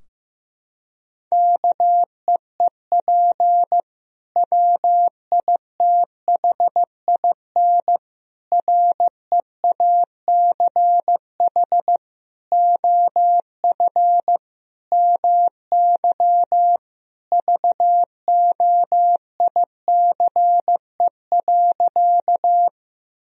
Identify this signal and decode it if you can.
Morse code, heard as a keyed tone, about 15 wpm, 700 Hz. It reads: KEEP WITHIN REACH OF MY VOICE.
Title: Sentences from Top 1000 Words - Review Part 4 - 15wpm